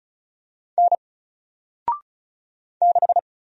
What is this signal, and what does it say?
Morse code, heard as a keyed tone, about 34 wpm, 700 Hz. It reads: N 6
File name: Single-Letter-Number-35wpm